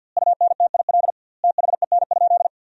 Morse code, heard as a keyed tone, about 52 wpm, 700 Hz. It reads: UNTIL THEN?
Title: 2x Word-Spacing - Sentences from Top 500 Words - 50wpm